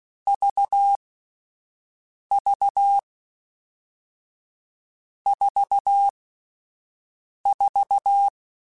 Morse code, heard as a keyed tone, about 16 wpm, spaced as for 5 wpm, 800 Hz. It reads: VV 44
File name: Difficult Combos / Difficult Combos-mp3s / AUV AUV4 @5